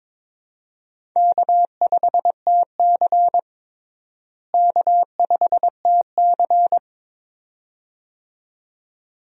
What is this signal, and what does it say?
Morse code, heard as a keyed tone, about 22 wpm, 700 Hz. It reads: K5TC K5TC
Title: USA Call Signs - 4 Characters - Repeated 2x - 22wpm